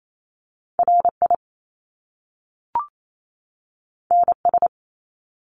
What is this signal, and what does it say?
Morse code, heard as a keyed tone, about 28 wpm, 700 Hz. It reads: RI NS
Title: Top Two Letter Combinations in Words 28wpm